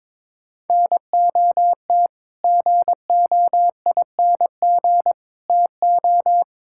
Morse code, heard as a keyed tone, about 22 wpm, 700 Hz. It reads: NOT GOING TO
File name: practice / 22wpm / Build A Phrase - 5-Word Phrases - 22wpm